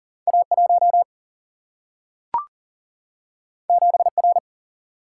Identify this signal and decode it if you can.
Morse code, heard as a keyed tone, about 40 wpm, 700 Hz. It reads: A1 7R
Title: Single-Letter Single-Number 40wpm